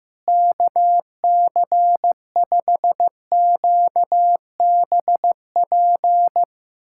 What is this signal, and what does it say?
Morse code, heard as a keyed tone, about 15 wpm, 700 Hz. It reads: KC5QBP
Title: USA Callsign DE Callsign - 6 Characters - 15wpm